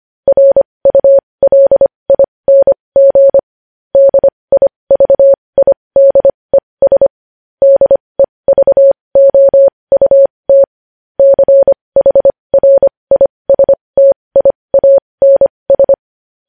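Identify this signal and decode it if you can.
Morse code, heard as a keyed tone, about 25 wpm, spaced as for 20 wpm, 550 Hz. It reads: RULING DIVIDES DEVOUT CHRISTIANS
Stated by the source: Morse code news headlines sent at 20.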